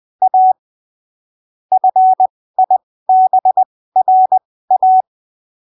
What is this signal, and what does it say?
Morse code, heard as a keyed tone, about 20 wpm, spaced as for 15 wpm, 750 Hz. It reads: A FIBRA
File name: arquivos / texto / texto 11 20ppm